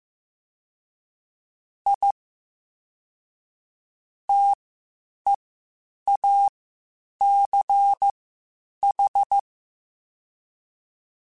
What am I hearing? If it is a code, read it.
Morse code, heard as a keyed tone, about 15 wpm, spaced as for 8 wpm, 800 Hz. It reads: I TEACH